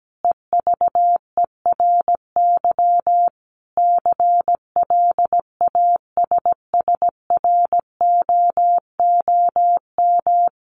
Morse code, heard as a keyed tone, about 17 wpm, 700 Hz. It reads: EVERY CLASSROOM
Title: Sentences from Top 1500 Words 17wpm